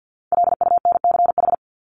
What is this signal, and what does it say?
Morse code, heard as a keyed tone, about 48 wpm, 700 Hz. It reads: LUNCH
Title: Top 1751-2000 Words 50wpm